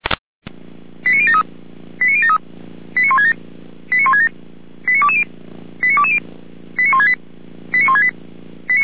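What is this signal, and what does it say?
Five-tone selective call (ZVEI), heard as quick runs of five tones, tones of about 70 ms. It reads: 89073 89073 89167 89167 89220 89220 89168 89168
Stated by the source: ZVEI five-tone selective call 89022, received by SDR, from the Feuerwehr Orlamünde archive.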